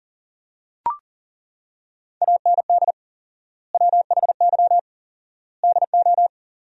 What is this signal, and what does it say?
Morse code, heard as a keyed tone, about 40 wpm, 700 Hz. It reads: AND WHY DO
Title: 4x Word-Spacing - Sentences from Top 400 Words - 40wpm